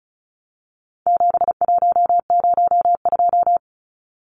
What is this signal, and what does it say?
Morse code, heard as a keyed tone, about 34 wpm, 700 Hz. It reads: 7102
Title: Numbers - 4-Digit 35wpm